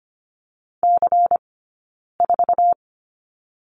Morse code, heard as a keyed tone, about 25 wpm, 700 Hz. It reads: C 4
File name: course / 25:10wpm / 092 Lesson 16d - Letter c - Two Characters 25wpm